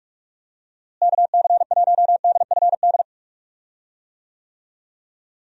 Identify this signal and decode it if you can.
Morse code, heard as a keyed tone, about 46 wpm, 700 Hz. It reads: KC1DFD
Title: USA Call Signs - 6 Characters - Repeated 2x - 45wpm